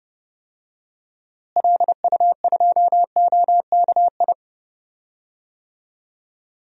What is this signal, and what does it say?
Morse code, heard as a keyed tone, about 29 wpm, 700 Hz. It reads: LU2OKI
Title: DXCC-100 Call Signs - 6 Characters - Repeated 3x - 30wpm